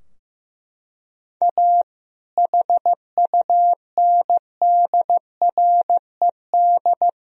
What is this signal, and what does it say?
Morse code, heard as a keyed tone, about 16 wpm, 700 Hz. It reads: A HUNDRED
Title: Sentences from Top 1000 Words - Review Part 2 - 15wpm